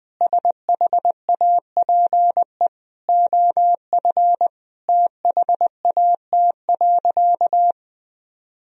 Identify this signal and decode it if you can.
Morse code, heard as a keyed tone, about 20 wpm, 700 Hz. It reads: SHAPE OF THAT.
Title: Sentences from Top 1000 Words - Review Part 3 - 20wpm